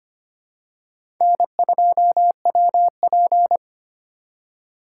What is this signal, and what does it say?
Morse code, heard as a keyed tone, about 25 wpm, 700 Hz. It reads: N2WP
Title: USA Call Signs - 4 Characters - 4x Character Spacing - 25wpm